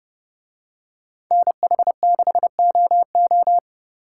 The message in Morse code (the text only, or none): NH6OO